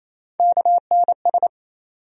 Morse code, heard as a keyed tone, about 28 wpm, 700 Hz. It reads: KNS
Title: USA Call Sign Postfixes - 3 Letters 28wpm